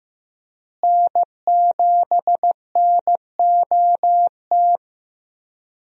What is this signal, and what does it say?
Morse code, heard as a keyed tone, about 15 wpm, 700 Hz. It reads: N7NOT
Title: USA Call Signs - 5 Characters - Repeated 3x - 15wpm